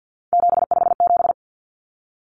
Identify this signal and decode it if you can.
Morse code, heard as a keyed tone, about 50 wpm, 700 Hz. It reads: 757
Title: Common Cut Numbers - 3-Digit 50wpm